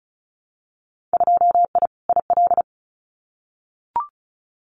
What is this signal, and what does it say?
Morse code, heard as a keyed tone, about 36 wpm, 700 Hz. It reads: 2I IL
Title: Winter Field Day Exchanges Encore 35wpm